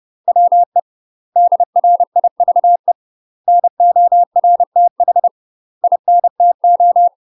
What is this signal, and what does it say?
Morse code, heard as a keyed tone, about 30 wpm, 700 Hz. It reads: WE DRIVE NORTH INTO
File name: UK MASTER QSOs TRAINING / 30WPM / UK MASTER QSOs 0022